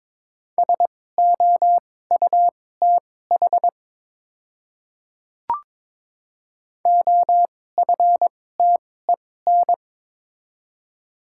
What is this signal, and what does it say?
Morse code, heard as a keyed tone, about 22 wpm, spaced as for 16 wpm, 700 Hz. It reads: SOUTH OFTEN
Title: 2x Character Spacing - Top 500 Words - 22wpm